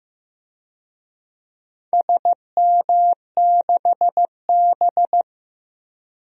Morse code, heard as a keyed tone, about 16 wpm, 700 Hz. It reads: SM6B
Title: DXCC-100 Call Signs - 4 Characters - Repeated 3x - 15wpm